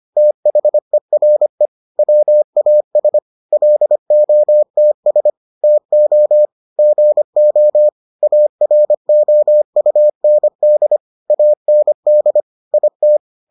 Morse code, heard as a keyed tone, about 25 wpm, 600 Hz. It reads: THERE WAS LOTS TO GO AROUND AND IT